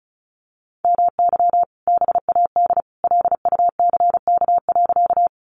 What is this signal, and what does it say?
Morse code, heard as a keyed tone, about 34 wpm, 700 Hz. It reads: MY BAD LUCK.
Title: Sentences from Top 1500 Words 35wpm